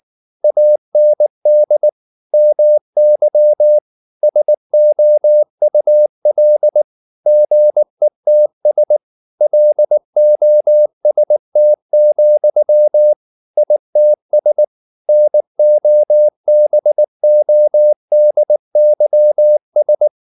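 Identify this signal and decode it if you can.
Morse code, heard as a keyed tone, about 19 wpm, 600 Hz. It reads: AND MY SOUL GETS LOST, ITS NOBODYS
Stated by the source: Morse code at 19 wpm from CW med Gnister, reading Ken Follett's novel Never in English.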